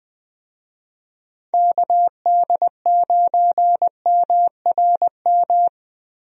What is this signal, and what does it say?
Morse code, heard as a keyed tone, about 21 wpm, 700 Hz. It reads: KD9MRM
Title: USA Call Signs - 6 Characters - Repeated 2x - 20wpm